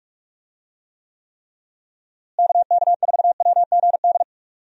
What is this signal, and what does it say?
Morse code, heard as a keyed tone, about 46 wpm, 700 Hz. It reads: KK4WGD
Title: USA Call Signs - 6 Characters - Repeated 3x - 45wpm